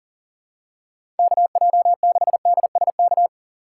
Morse code, heard as a keyed tone, about 37 wpm, 700 Hz. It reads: KJ6DSK
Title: USA Call Signs - 6 Characters - 2x Character Spacing - 40wpm